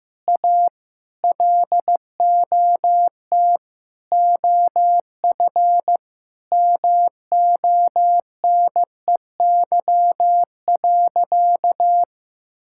Morse code, heard as a keyed tone, about 15 wpm, 700 Hz. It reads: A LOT OF MONEY.
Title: Sentences from Top 1000 Words - Review Part 5 - 15wpm